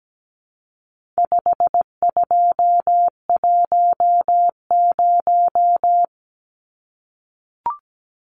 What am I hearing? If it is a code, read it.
Morse code, heard as a keyed tone, about 17 wpm, 700 Hz. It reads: 5210
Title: Numbers - 4-Digit 17wpm